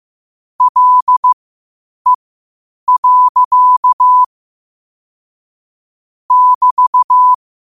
Morse code, heard as a keyed tone, about 15 wpm, spaced as for 8 wpm, 1000 Hz. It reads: LE. <BT>
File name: qotdfiles / 8wpm